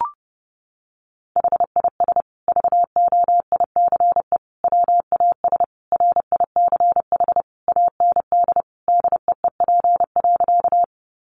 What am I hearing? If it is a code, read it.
Morse code, heard as a keyed tone, about 28 wpm, 700 Hz. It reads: HIS VOICE WAS RICH AND DEEP.
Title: Sentences from Top 1100 Words 30wpm